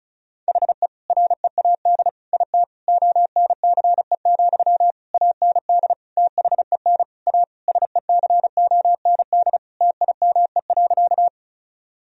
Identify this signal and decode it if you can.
Morse code, heard as a keyed tone, about 36 wpm, 700 Hz. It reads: HE READ IT ONCE, AND THEN A SECOND TIME.
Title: Sentences from Top 1000 Words - Review Part 1 - 35wpm